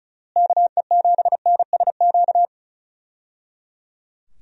Morse code, heard as a keyed tone, about 36 wpm, 700 Hz. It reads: KE7NSQ
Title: USA Call Signs - 6 Characters - Repeated 3x - 35wpm